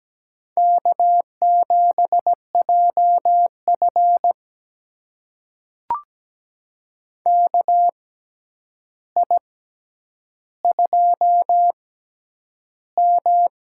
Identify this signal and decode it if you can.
Morse code, heard as a keyed tone, about 17 wpm, spaced as for 9 wpm, 700 Hz. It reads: K7JF KI2M
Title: USA Call Signs - 4 Characters - 6x Character Spacing - 17wpm